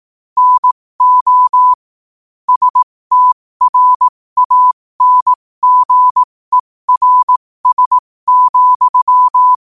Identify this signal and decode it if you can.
Morse code, heard as a keyed tone, about 18 wpm, spaced as for 15 wpm, 1000 Hz. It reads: NO STRANGERS,
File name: qotdfiles / 15wpm